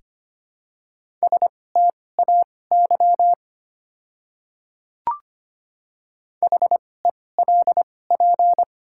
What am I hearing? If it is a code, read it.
Morse code, heard as a keyed tone, about 25 wpm, spaced as for 18 wpm, 700 Hz. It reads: STAY HELP
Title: 2x Character Spacing - Top 500 Words - 25wpm